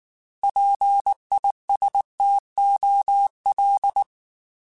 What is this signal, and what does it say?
Morse code, heard as a keyed tone, about 19 wpm, 800 Hz. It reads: PISTOL